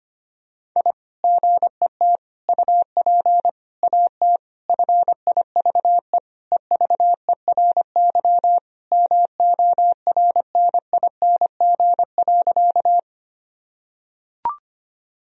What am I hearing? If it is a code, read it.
Morse code, heard as a keyed tone, about 25 wpm, 700 Hz. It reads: I GET UP AT FIVE EVERY MORNING.
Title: Sentences from Top 1000 Words - Review Part 2 - 25wpm